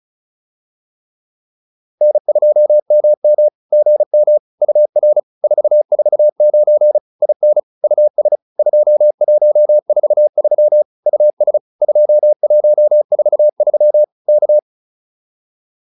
Morse code, heard as a keyed tone, about 34 wpm, 600 Hz. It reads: N2MM GM UR 449 IN US 2143 US 2143 K